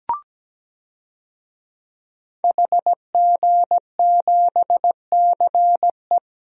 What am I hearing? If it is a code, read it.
Morse code, heard as a keyed tone, about 17 wpm, 700 Hz. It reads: HG7CE
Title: DXCC-100 Call Signs - 5 Characters - Repeated 3x - 17wpm